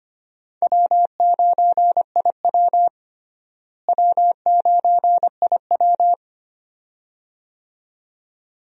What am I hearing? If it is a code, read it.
Morse code, heard as a keyed tone, about 25 wpm, 700 Hz. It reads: W9IW W9IW
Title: USA Call Signs - 4 Characters - Repeated 2x - 25wpm